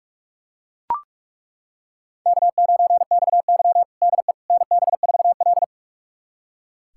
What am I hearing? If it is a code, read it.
Morse code, heard as a keyed tone, about 41 wpm, 700 Hz. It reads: K9XY DE NB4L
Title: USA Callsign DE Callsign - 4 Characters - 45wpm